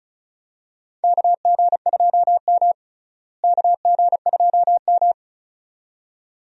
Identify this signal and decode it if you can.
Morse code, heard as a keyed tone, about 36 wpm, 700 Hz. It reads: KG2M KG2M
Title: USA Call Signs - 4 Characters - Repeated 2x - 35wpm